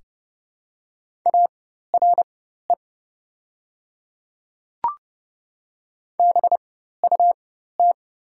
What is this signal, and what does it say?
Morse code, heard as a keyed tone, about 30 wpm, spaced as for 14 wpm, 700 Hz. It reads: ARE BUT
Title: 4x Character Spacing - Top 300 Words - 30wpm